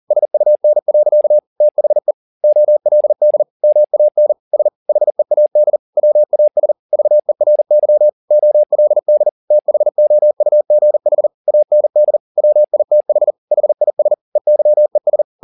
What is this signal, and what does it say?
Morse code, heard as a keyed tone, about 40 wpm, 600 Hz. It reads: SUN. THE OLD MAN S HEAD WAS VERY OLD THOUGH AND WITH HIS EYES